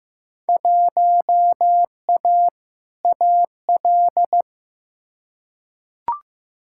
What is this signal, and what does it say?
Morse code, heard as a keyed tone, about 15 wpm, 700 Hz. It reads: 1A AL
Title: Field Day Exchanges - 15wpm